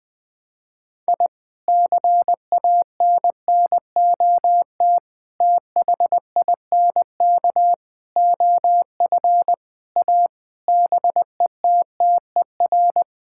Morse code, heard as a keyed tone, about 20 wpm, 700 Hz. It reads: I CANNOT THINK OF A BETTER
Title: Sentences with 5-Word Phrases - 20wpm